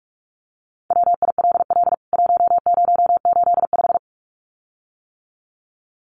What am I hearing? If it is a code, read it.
Morse code, heard as a keyed tone, about 44 wpm, 700 Hz. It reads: WILL 1085
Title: CWT Exchange 45wpm